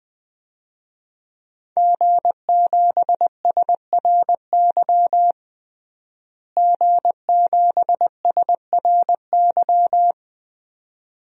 Morse code, heard as a keyed tone, about 20 wpm, 700 Hz. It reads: G7SRY G7SRY
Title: DXCC-100 Call Signs - 5 Characters - Repeated 2x - 20wpm